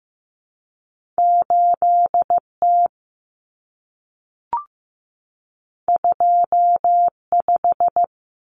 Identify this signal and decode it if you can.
Morse code, heard as a keyed tone, about 15 wpm, 700 Hz. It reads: 8T 25
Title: Common Cut Numbers - 2-Digit 15wpm